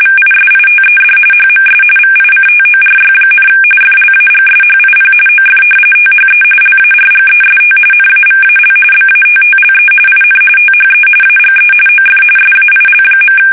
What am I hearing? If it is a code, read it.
RTTY, about 45.45 baud. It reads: VISITED HMS VICTORY, FLAGSHIP OF THE LARGEST NAVAL BATTLE IN THE AGE OF SAIL.
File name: fsk850 / battleshipnj